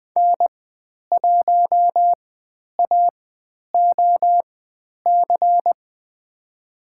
Morse code, heard as a keyed tone, about 20 wpm, spaced as for 10 wpm, 700 Hz. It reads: N1AOC
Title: 109 Lesson 18g - Letter m - Call Signs 20wpm